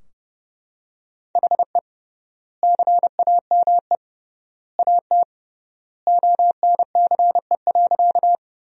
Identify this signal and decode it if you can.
Morse code, heard as a keyed tone, about 31 wpm, 700 Hz. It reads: HE CAME AT ONCE.
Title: 3x Word-Spacing - Sentences from Top 300 Words - 30wpm